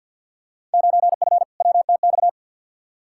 Morse code, heard as a keyed tone, about 52 wpm, 700 Hz. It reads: 9F WTX
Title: Field Day Exchanges Encore - 50wpm